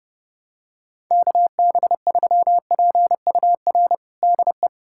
Morse code, heard as a keyed tone, about 30 wpm, 700 Hz. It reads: KB3PUR DE
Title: USA Callsign DE Callsign - 6 Characters - 30wpm